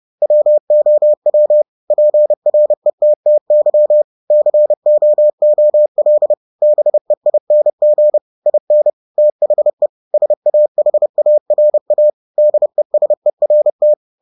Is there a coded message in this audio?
Yes: Morse code from 0.2 to 13.9 s, heard as a keyed tone, about 30 wpm, 600 Hz. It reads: WOW PRETTY COOL BEING IN THE SAHARA DESERT